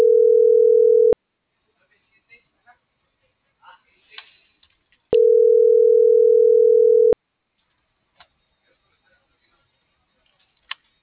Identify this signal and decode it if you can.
Ringback tone (North America), 440 + 480 Hz.
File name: CENTRAL MORENA / out ruth.skyangel.24 - .5513